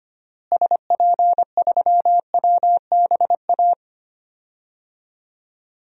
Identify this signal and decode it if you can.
Morse code, heard as a keyed tone, about 25 wpm, 700 Hz. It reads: SP3WBA
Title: DXCC-100 Call Signs - 6 Characters - Repeated 2x - 25wpm